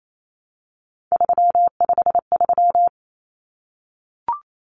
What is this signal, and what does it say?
Morse code, heard as a keyed tone, about 28 wpm, 700 Hz. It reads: 353
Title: Common Cut Numbers - 3-Digit 28wpm